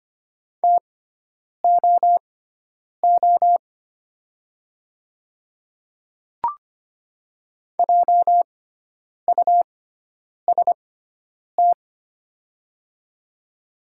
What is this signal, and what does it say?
Morse code, heard as a keyed tone, about 25 wpm, spaced as for 9 wpm, 700 Hz. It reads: TOO JUST
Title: 6x Character Spacing - Top 200 Words - 25wpm